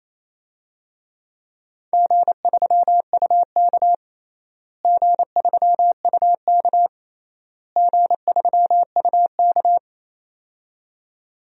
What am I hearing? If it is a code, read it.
Morse code, heard as a keyed tone, about 28 wpm, 700 Hz. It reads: G3UK G3UK G3UK
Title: DXCC-100 Call Signs - 4 Characters - Repeated 3x - 28wpm